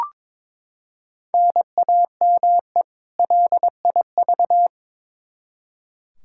Morse code, heard as a keyed tone, about 21 wpm, 700 Hz. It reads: NAME LIV
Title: Practice QSO - Exchange 1 - Beginning - 22wpm